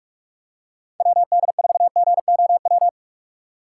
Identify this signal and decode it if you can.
Morse code, heard as a keyed tone, about 44 wpm, 700 Hz. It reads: WD4GOW
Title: USA Call Signs - 6 Characters - 4x Character Spacing - 45wpm